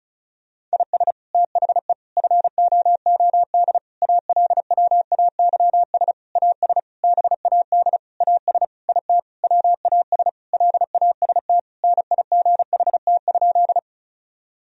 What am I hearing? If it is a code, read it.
Morse code, heard as a keyed tone, about 36 wpm, 700 Hz. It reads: IS THE FOOD ALWAYS AS BAD AS IT WAS LAST NIGHT?
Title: Sentences from Top 1000 Words - Review Part 1 - 35wpm